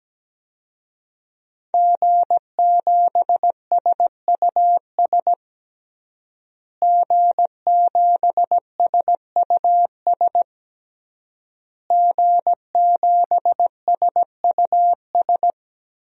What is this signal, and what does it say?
Morse code, heard as a keyed tone, about 17 wpm, 700 Hz. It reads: G7SUS G7SUS G7SUS
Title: DXCC-100 Call Signs - 5 Characters - Repeated 3x - 17wpm